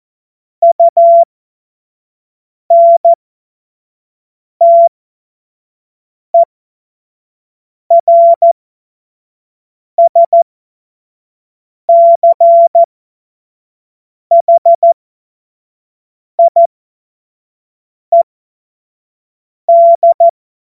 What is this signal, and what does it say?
Morse code, heard as a keyed tone, about 14 wpm, spaced as for 5 wpm, 680 Hz. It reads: UNTERSCHIED
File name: Morsekurs / Witze / Witz 14.November 2023 25 F70